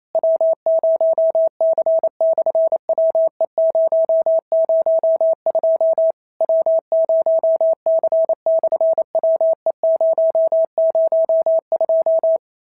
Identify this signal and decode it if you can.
Morse code, heard as a keyed tone, about 25 wpm, 650 Hz. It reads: W0C/WE002 W0C/WE002
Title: SOTA QSO Practice 28wpm